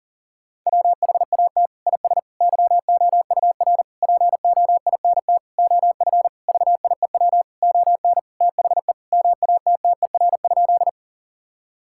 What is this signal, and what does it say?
Morse code, heard as a keyed tone, about 40 wpm, 700 Hz. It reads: WHAT IS YOUR POINT OF VIEW ON THE MATTER?
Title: Sentences from Top 1000 Words - Review Part 6 - 40wpm